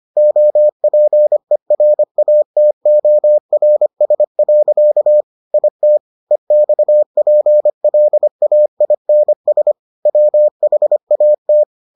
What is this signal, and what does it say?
Morse code, heard as a keyed tone, about 25 wpm, 600 Hz. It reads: OPERATORS. IT EXPLAINS WHAT